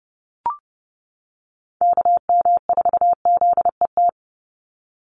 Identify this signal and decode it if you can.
Morse code, heard as a keyed tone, about 30 wpm, 700 Hz. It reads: KM4ZET
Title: USA Call Signs 30wpm